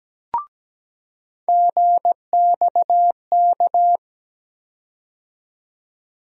Morse code, heard as a keyed tone, about 17 wpm, 700 Hz. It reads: GXK
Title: USA Call Sign Postfixes - 3 Letters 17wpm